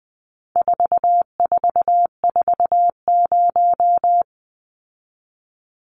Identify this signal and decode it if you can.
Morse code, heard as a keyed tone, about 20 wpm, 700 Hz. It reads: 4440
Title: Numbers - 4-Digit 20wpm